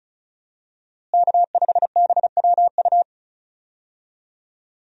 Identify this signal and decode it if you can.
Morse code, heard as a keyed tone, about 36 wpm, 700 Hz. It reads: K5BWU
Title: USA Call Signs - 5 Characters - Repeated 2x - 35wpm